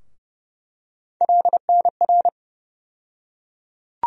Morse code, heard as a keyed tone, about 30 wpm, 700 Hz. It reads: LNR
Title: USA Call Sign Postfixes - 3 Letters 30wpm